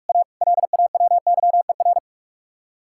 Morse code, heard as a keyed tone, about 46 wpm, 700 Hz. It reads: A LAWYER
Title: Speed-Racing Common QSO Phrases - 45wpm